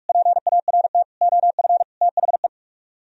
Morse code, heard as a keyed tone, about 45 wpm, 690 Hz. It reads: PART OF THE